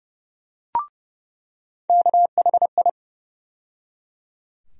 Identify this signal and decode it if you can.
Morse code, heard as a keyed tone, about 30 wpm, 700 Hz. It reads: KHI
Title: USA Call Sign Postfixes - 3 Letters 30wpm